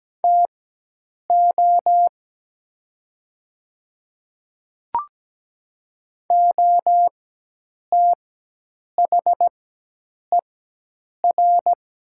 Morse code, heard as a keyed tone, about 17 wpm, spaced as for 8 wpm, 700 Hz. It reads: TO OTHER
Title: 4x Character Spacing - Top 100 Words - 17wpm